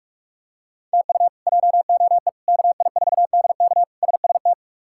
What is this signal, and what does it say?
Morse code, heard as a keyed tone, about 46 wpm, 700 Hz. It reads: TU JOE KI4DK SST
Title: K1USN SST QSO Practice - 45wpm